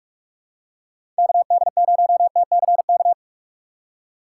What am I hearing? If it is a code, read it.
Morse code, heard as a keyed tone, about 42 wpm, 700 Hz. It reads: KD0TCK
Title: USA Call Signs - 6x Character Spacing - 45wpm